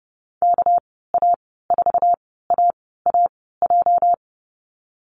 Morse code, heard as a keyed tone, about 32 wpm, spaced as for 17 wpm, 700 Hz. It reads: KA4AAJ
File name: Lesson 37g - Letter x - Call Signs 30wpm